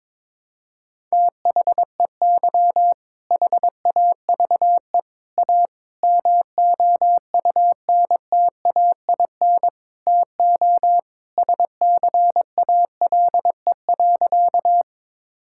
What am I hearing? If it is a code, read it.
Morse code, heard as a keyed tone, about 22 wpm, 700 Hz. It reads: THEY HAVE A MOUNTAIN TO SCALE.
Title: Sentences from Top 1000 Words - Review Part 5 - 22wpm